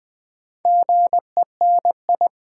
Morse code, heard as a keyed tone, about 20 wpm, 700 Hz. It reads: GENI
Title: Build A Word - 2 Letters At A Time - 6-Letter Words - 20wpm